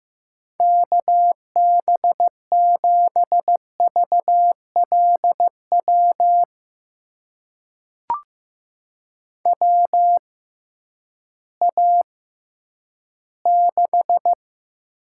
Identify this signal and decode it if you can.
Morse code, heard as a keyed tone, about 15 wpm, spaced as for 10 wpm, 700 Hz. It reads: KB7VLW WA6
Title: USA Call Signs - 6 Characters - 6x Character Spacing - 15wpm